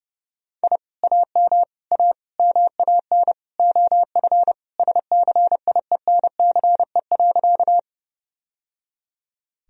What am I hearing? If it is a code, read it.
Morse code, heard as a keyed tone, about 30 wpm, 700 Hz. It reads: I AM A MAN OF SCIENCE.
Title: Sentences from Top 1000 Words - Review Part 2 - 30wpm